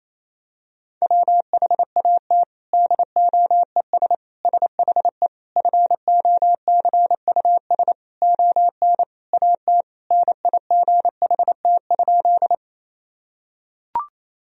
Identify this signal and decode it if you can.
Morse code, heard as a keyed tone, about 28 wpm, 700 Hz. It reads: WHAT DOES SHE FOCUS ON AT NIGHT?
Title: Sentences from Top 1000 Words - Review Part 4 - 28wpm